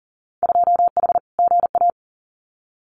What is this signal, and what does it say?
Morse code, heard as a keyed tone, about 41 wpm, 700 Hz. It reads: 2H GA
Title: Winter Field Day Exchanges 40wpm